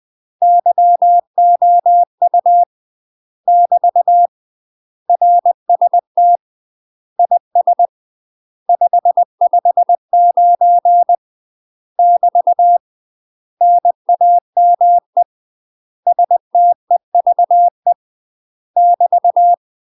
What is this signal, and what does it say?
Morse code, heard as a keyed tone, about 20 wpm, 700 Hz. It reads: YOU <BT> RST IS 559 <BT> NAME STEVE <BT>
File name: UK MASTER QSOs TRAINING / 20WPM / UK MASTER QSOs 0042